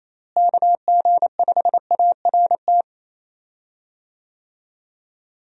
Morse code, heard as a keyed tone, about 28 wpm, 700 Hz. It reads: KG5ART
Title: USA Call Signs - 6 Characters - Repeated 2x - 28wpm